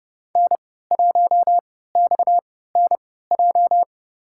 Morse code, heard as a keyed tone, about 30 wpm, spaced as for 17 wpm, 700 Hz. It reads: N1XNJ
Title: USA Call Signs - 5 Characters - 3x Character Spacing - 30wpm